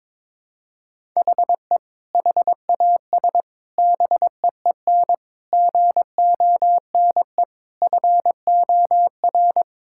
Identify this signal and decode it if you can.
Morse code, heard as a keyed tone, about 22 wpm, 700 Hz. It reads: HE HAS BEEN GONE FOR
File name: practice / 22wpm / Sentences with 4-Word Phrases - 22wpm